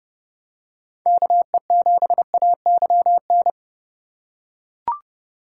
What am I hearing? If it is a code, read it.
Morse code, heard as a keyed tone, about 32 wpm, 700 Hz. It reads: KE7AYN